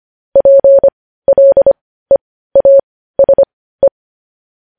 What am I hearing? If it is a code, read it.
Morse code, heard as a keyed tone, about 25 wpm, spaced as for 15 wpm, 550 Hz. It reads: PLEASE
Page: news headlines in morse code at 15 wpm